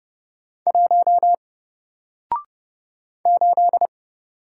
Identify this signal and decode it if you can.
Morse code, heard as a keyed tone, about 30 wpm, 700 Hz. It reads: 1 8
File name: Single-Letter-Number-30wpm